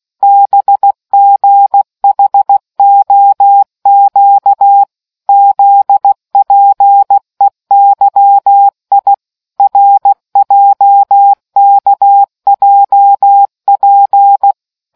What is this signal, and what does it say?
Morse code, heard as a keyed tone, about 16 wpm, 800 Hz. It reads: BGHOQ ZPEYI RJKJP